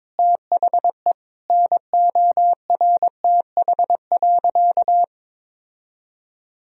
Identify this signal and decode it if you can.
Morse code, heard as a keyed tone, about 22 wpm, 700 Hz. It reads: THE NORTH.
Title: Sentences from Top 1000 Words - Review Part 2 - 22wpm